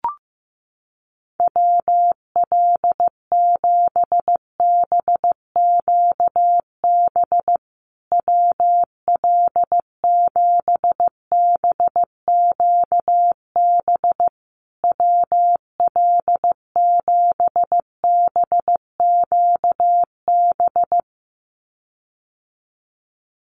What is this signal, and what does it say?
Morse code, heard as a keyed tone, about 15 wpm, 700 Hz. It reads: WL7BQB WL7BQB WL7BQB
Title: USA Call Signs Repeated 3x 15wpm